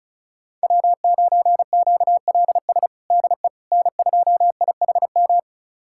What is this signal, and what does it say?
Morse code, heard as a keyed tone, about 36 wpm, 700 Hz. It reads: W9QLS DE N2IHM
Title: USA Callsign DE Callsign - 5 Characters - 35wpm